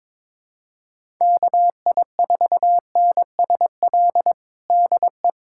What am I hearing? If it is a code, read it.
Morse code, heard as a keyed tone, about 22 wpm, 700 Hz. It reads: KI4NSL DE
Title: USA Callsign DE Callsign - 6 Characters - 22wpm